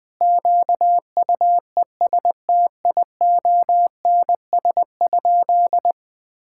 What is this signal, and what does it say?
Morse code, heard as a keyed tone, about 20 wpm, 700 Hz. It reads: QUESTIONS?
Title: Sentences with 4-Word Phrases - Spoken Beginning - 20wpm